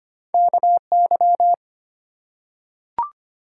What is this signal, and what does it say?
Morse code, heard as a keyed tone, about 25 wpm, 700 Hz. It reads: KY
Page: morse-code-ninja-practice-25wpm